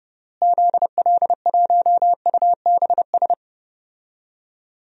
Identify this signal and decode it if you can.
Morse code, heard as a keyed tone, about 30 wpm, 700 Hz. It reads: ZL1UBS
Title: DXCC-100 Call Signs - 6 Characters - Repeated 3x - 30wpm